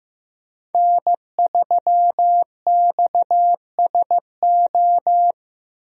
Morse code, heard as a keyed tone, about 15 wpm, 700 Hz. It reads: N3XSO
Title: USA Call Signs - 5 Characters - Repeated 3x - 15wpm